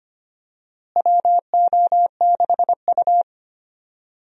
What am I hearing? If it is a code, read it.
Morse code, heard as a keyed tone, about 25 wpm, 700 Hz. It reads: WO6U
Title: USA Call Signs - 4 Characters - Repeated 3x - 25wpm